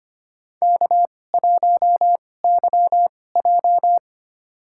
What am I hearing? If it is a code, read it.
Morse code, heard as a keyed tone, about 25 wpm, spaced as for 18 wpm, 700 Hz. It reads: K1YJ
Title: USA Call Signs - 4 Characters - 2x Character Spacing - 25wpm